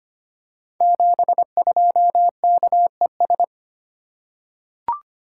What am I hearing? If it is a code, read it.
Morse code, heard as a keyed tone, about 25 wpm, 700 Hz. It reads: 72KES
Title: Code Groups - 5 Letter:Numbers - 25wpm